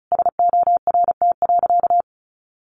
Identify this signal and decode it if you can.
Morse code, heard as a keyed tone, about 34 wpm, 700 Hz. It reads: SORT.
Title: Sentences from Top 1500 Words 35wpm